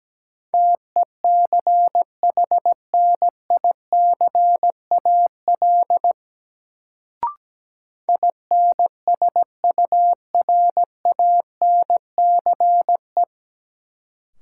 Morse code, heard as a keyed tone, about 17 wpm, 700 Hz. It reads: TECHNICAL INSURANCE
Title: Sentences from Top 1100 Words - New Word Drill Alt - 17wpm